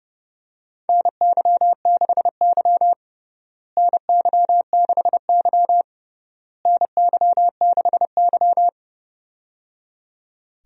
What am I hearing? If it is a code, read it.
Morse code, heard as a keyed tone, about 30 wpm, 700 Hz. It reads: NY6Y NY6Y NY6Y
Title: USA Call Signs - 4 Characters - Repeated 3x - 30wpm